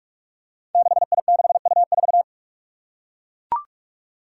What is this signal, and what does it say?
Morse code, heard as a keyed tone, about 42 wpm, 700 Hz. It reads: 6I6U4
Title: Five Challenging Letter-Number Combinations 45wpm